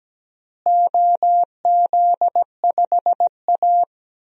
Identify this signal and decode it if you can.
Morse code, heard as a keyed tone, about 17 wpm, 700 Hz. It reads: OZ5A
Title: DXCC-100 Call Signs - 4 Characters - Repeated 3x - 17wpm